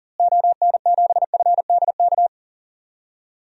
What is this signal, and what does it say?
Morse code, heard as a keyed tone, about 39 wpm, 700 Hz. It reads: ON7FDK
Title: DXCC-100 Call Signs - 6 Characters - Repeated 3x - 40wpm